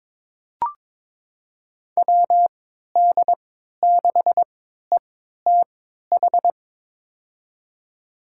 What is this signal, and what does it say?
Morse code, heard as a keyed tone, about 21 wpm, spaced as for 12 wpm, 700 Hz. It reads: WD6ETH
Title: USA Call Signs - 6 Characters - 3x Character Spacing - 22wpm